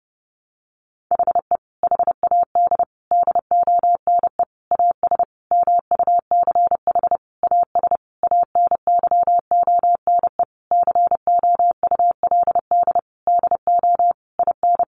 Morse code, heard as a keyed tone, about 30 wpm, 700 Hz. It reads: HE HAD DONE AS MUCH AS ANYONE COULD DO IN
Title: Sentences from Top 1200 Words 30wpm